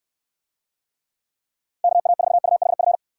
Morse code, heard as a keyed tone, about 68 wpm, 700 Hz. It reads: KA4UHF